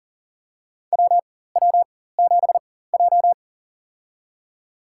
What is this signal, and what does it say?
Morse code, heard as a keyed tone, about 40 wpm, spaced as for 19 wpm, 700 Hz. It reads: WW7J
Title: USA Call Signs - 4x Character Spacing - 40wpm